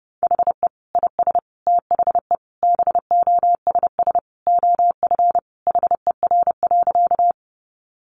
Morse code, heard as a keyed tone, about 30 wpm, 700 Hz. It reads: HE IS THE BOSS OF HER.
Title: Sentences from Top 700 Words 30wpm